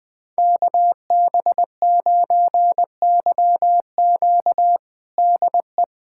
Morse code, heard as a keyed tone, about 19 wpm, 700 Hz. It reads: KB9YQ DE
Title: USA Callsign DE Callsign 20wpm